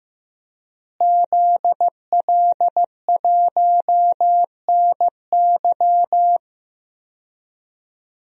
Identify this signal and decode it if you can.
Morse code, heard as a keyed tone, about 15 wpm, 700 Hz. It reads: ZL1NY
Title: DXCC-100 Call Signs - 5 Characters - Repeated 3x - 15wpm